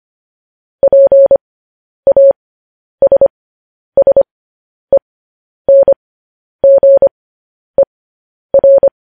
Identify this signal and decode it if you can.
Morse code, heard as a keyed tone, about 25 wpm, spaced as for 10 wpm, 550 Hz. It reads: PASSENGER
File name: podcast / News headlines-1x-25x10